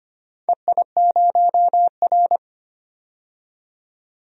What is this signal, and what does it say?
Morse code, heard as a keyed tone, about 25 wpm, 700 Hz. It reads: EI0R
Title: DXCC-100 Call Signs - 4 Characters - 25wpm